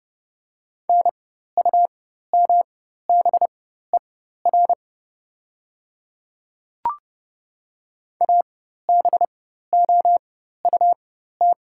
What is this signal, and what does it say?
Morse code, heard as a keyed tone, about 32 wpm, spaced as for 14 wpm, 700 Hz. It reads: NUMBER ABOUT
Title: 4x Character Spacing - Top 500 Words - 30wpm